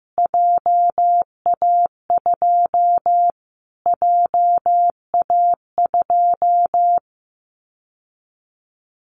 Morse code, heard as a keyed tone, about 15 wpm, 700 Hz. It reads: JA2 JA2
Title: Callsign Prefixes for Top 25 DXCC - Repeated 3x - 15wpm